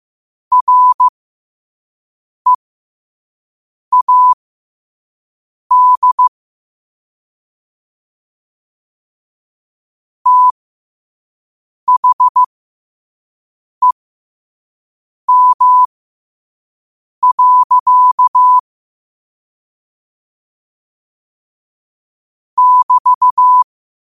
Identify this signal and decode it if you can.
Morse code, heard as a keyed tone, about 15 wpm, spaced as for 5 wpm, 1000 Hz. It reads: READ THEM. <BT>
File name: qotdfiles / 5wpm